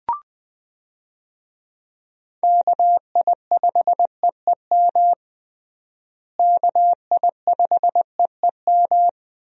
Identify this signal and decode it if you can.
Morse code, heard as a keyed tone, about 20 wpm, 700 Hz. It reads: KI5EEM KI5EEM
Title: USA Call Signs - 6 Characters - Repeated 2x - 20wpm